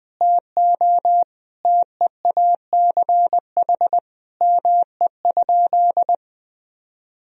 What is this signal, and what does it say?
Morse code, heard as a keyed tone, about 20 wpm, 700 Hz. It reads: TO TEACH ME?
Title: Sentences from Top 1000 Words - Review Part 2 - 20wpm